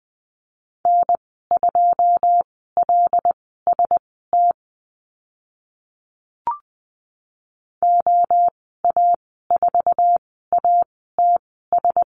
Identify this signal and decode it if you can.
Morse code, heard as a keyed tone, about 20 wpm, spaced as for 15 wpm, 700 Hz. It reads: N2LST OA4ATS
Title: 082 Lesson 14g - Number 2 - Call Signs 20wpm